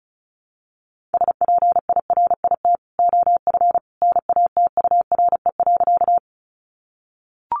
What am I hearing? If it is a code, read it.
Morse code, heard as a keyed tone, about 34 wpm, 700 Hz. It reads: SPIRIT OF NATURE.
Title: Sentences from Top 1400 Words 35wpm